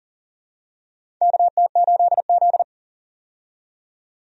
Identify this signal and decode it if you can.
Morse code, heard as a keyed tone, about 40 wpm, 700 Hz. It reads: KT8Z
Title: USA Call Signs - 4 Characters - 40wpm